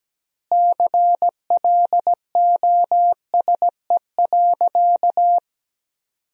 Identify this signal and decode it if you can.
Morse code, heard as a keyed tone, about 17 wpm, 700 Hz. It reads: CLOSE.
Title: 4x Word-Spacing - Sentences from Top 300 Words - 17wpm